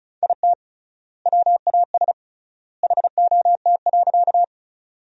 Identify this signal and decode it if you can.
Morse code, heard as a keyed tone, about 36 wpm, 700 Hz. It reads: IT WAS HOT.
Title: 3x Word-Spacing - Sentences from Top 500 Words - 35wpm